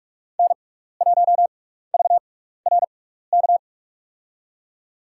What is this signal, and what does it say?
Morse code, heard as a keyed tone, about 44 wpm, spaced as for 15 wpm, 700 Hz. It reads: N1VRK
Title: USA Call Signs - 6x Character Spacing - 45wpm